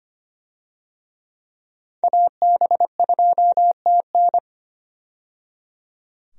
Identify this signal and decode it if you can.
Morse code, heard as a keyed tone, about 25 wpm, 700 Hz. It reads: AB2TN
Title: USA Call Signs - 5 Characters - 25wpm